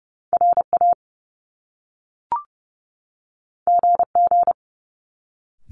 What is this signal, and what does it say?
Morse code, heard as a keyed tone, about 30 wpm, 700 Hz. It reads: RA GG
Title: Top Two Letter Combinations in Words 30wpm